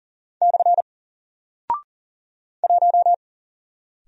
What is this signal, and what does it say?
Morse code, heard as a keyed tone, about 39 wpm, 700 Hz. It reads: / 1
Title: Single Character (Letter:Number:Punctuation:Prosign) 40wpm